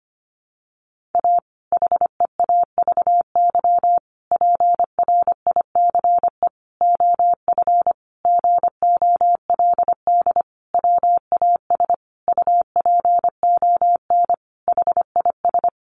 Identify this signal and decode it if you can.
Morse code, heard as a keyed tone, about 25 wpm, 700 Hz. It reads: A HEAVY PRICE OF GOLD WAS UPON HIS
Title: Sentences from Top 1100 Words 25wpm